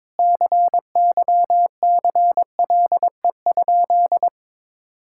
Morse code, heard as a keyed tone, about 22 wpm, 700 Hz. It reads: CYCLE?
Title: Sentences from Top 1000 Words - Review Part 6 - 22wpm